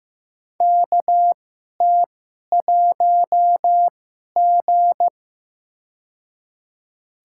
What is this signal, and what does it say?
Morse code, heard as a keyed tone, about 15 wpm, spaced as for 11 wpm, 700 Hz. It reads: KT1G
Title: USA Call Signs - 4 Characters - 2x Character Spacing - 15wpm